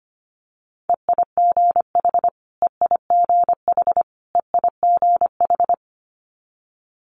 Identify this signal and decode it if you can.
Morse code, heard as a keyed tone, about 25 wpm, 700 Hz. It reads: EIGH EIGH EIGH
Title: Multi-letter Phonemes Repeated 3x 25wpm